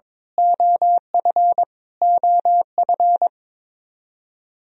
Morse code, heard as a keyed tone, about 21 wpm, 700 Hz. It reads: OF OF
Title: Top 100 Words Repeated 22wpm